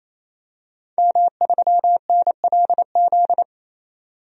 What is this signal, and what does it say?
Morse code, heard as a keyed tone, about 27 wpm, 700 Hz. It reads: M3NLZ
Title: DXCC-100 Call Signs - 5 Characters - 28wpm